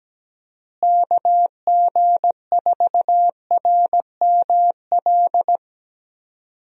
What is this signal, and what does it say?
Morse code, heard as a keyed tone, about 17 wpm, 700 Hz. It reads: KG4RML
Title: USA Call Signs - 6 Characters - Repeated 2x - 17wpm